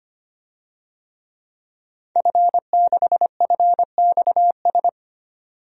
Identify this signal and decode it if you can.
Morse code, heard as a keyed tone, about 25 wpm, 700 Hz. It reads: F6FXS
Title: DXCC-100 Call Signs - 5 Characters - 25wpm